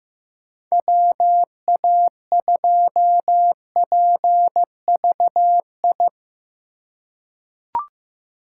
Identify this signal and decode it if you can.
Morse code, heard as a keyed tone, about 15 wpm, 700 Hz. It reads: WA2PVI
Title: USA Call Signs - 6 Characters - 2x Character Spacing - 15wpm